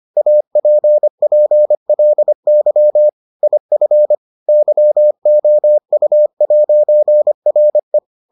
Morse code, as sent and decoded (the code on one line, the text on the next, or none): .- .--. .--. .-.. -.-- / .. ..-. / -.-- --- ..- .----. .-. .
APPLY IF YOU'RE